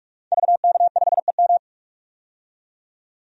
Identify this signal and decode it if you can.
Morse code, heard as a keyed tone, about 44 wpm, 700 Hz. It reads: VK5EM
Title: DXCC-100 Call Signs - 5 Characters - 45wpm